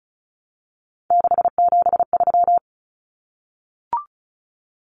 Morse code, heard as a keyed tone, about 34 wpm, 700 Hz. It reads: 673
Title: Common Cut Numbers - 3-Digit 35wpm